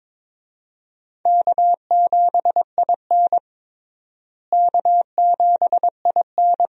Morse code, heard as a keyed tone, about 22 wpm, 700 Hz. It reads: K7IN K7IN
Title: USA Call Signs - 4 Characters - Repeated 2x - 22wpm